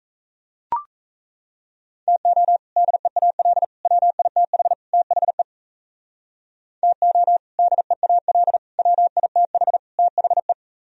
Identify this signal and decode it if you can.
Morse code, heard as a keyed tone, about 34 wpm, 700 Hz. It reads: TO DEAL WITH THE TO DEAL WITH THE
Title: Speed-Racing - Sets of 4 Words 28wpm